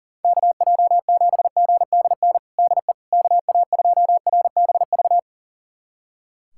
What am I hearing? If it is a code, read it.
Morse code, heard as a keyed tone, about 40 wpm, 700 Hz. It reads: KJ7GDN DE KA2RBV
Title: USA Callsign DE Callsign - 6 Characters - 40wpm